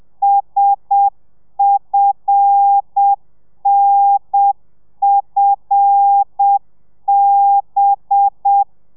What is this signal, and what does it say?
Morse code, heard as a keyed tone, about 7 wpm, 800 Hz. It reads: SFNFB
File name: telegrafia / i2tfj / Lez5 30